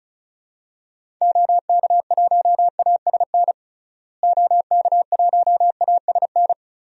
Morse code, heard as a keyed tone, about 36 wpm, 700 Hz. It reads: OK1ASN OK1ASN
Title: DXCC-100 Call Signs - 6 Characters - Repeated 3x - 35wpm